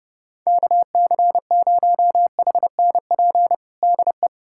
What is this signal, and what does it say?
Morse code, heard as a keyed tone, about 32 wpm, 700 Hz. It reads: KC0HNP DE
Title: USA Callsign DE Callsign 30wpm